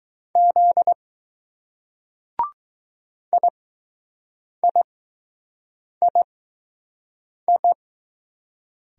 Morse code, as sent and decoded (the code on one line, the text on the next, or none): --.. / .. .. .. ..
Z IIII